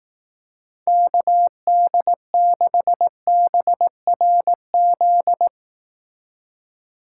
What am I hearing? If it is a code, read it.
Morse code, heard as a keyed tone, about 18 wpm, 700 Hz. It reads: KD6BRZ
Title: Speed Racing - USA 2x3 Call Signs 15wpm